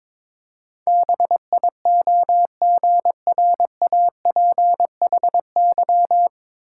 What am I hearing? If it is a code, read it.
Morse code, heard as a keyed tone, about 21 wpm, 700 Hz. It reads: BIOGRAPHY
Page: Morse Code Ninja practice 22wpm